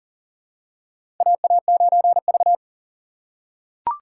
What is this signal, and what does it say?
Morse code, heard as a keyed tone, about 40 wpm, 700 Hz. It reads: AA9V
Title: USA Call Signs - 4 Characters - 3x Character Spacing - 40wpm